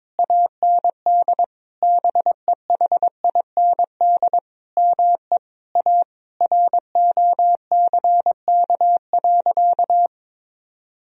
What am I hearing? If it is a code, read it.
Morse code, heard as a keyed tone, about 22 wpm, 700 Hz. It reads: AND BEHIND ME A ROCK.
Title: Sentences from Top 1000 Words - Review Part 2 - 22wpm